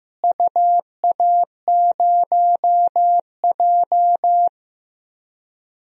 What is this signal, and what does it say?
Morse code, heard as a keyed tone, about 15 wpm, 700 Hz. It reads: UA0J
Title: DXCC-100 Call Signs - 4 Characters - Repeated 3x - 15wpm